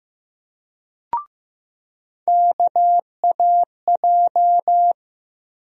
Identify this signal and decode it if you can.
Morse code, heard as a keyed tone, about 15 wpm, 700 Hz. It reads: KAJ
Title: USA Call Sign Postfixes - 3 Letters 15wpm